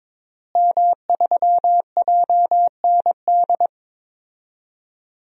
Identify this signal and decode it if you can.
Morse code, heard as a keyed tone, about 22 wpm, 700 Hz. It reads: M3JND
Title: DXCC-100 Call Signs - 5 Characters - Repeated 3x - 22wpm